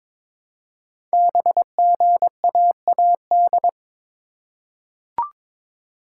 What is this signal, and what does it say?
Morse code, heard as a keyed tone, about 23 wpm, 700 Hz. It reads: BGAAD
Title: Code Groups - 5 Letters - 22wpm